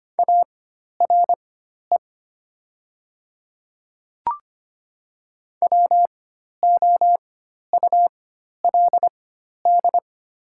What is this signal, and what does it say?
Morse code, heard as a keyed tone, about 25 wpm, spaced as for 12 wpm, 700 Hz. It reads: ARE WOULD
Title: 4x Character Spacing - Top 100 Words - 25wpm